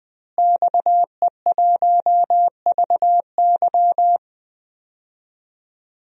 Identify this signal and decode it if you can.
Morse code, heard as a keyed tone, about 20 wpm, 700 Hz. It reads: XE1VY
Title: DXCC-100 Call Signs - 5 Characters - Repeated 2x - 20wpm